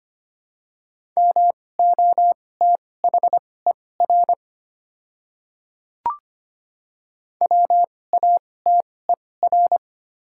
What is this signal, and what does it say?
Morse code, heard as a keyed tone, about 25 wpm, spaced as for 18 wpm, 700 Hz. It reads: MOTHER WATER
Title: 2x Character Spacing - Top 200 Words - 25wpm